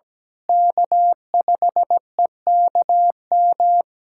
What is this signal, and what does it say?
Morse code, heard as a keyed tone, about 17 wpm, 700 Hz. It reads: K5EKM
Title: USA Callsign DE Callsign - 5 Characters - 17wpm